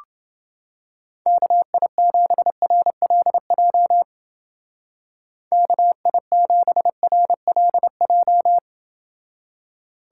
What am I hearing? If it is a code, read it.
Morse code, heard as a keyed tone, about 29 wpm, 700 Hz. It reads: KI7RLJ KI7RLJ